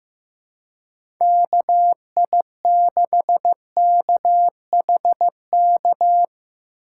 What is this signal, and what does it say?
Morse code, heard as a keyed tone, about 15 wpm, 700 Hz. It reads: KI6KHK
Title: USA Call Signs - 3x Character Spacing - 15wpm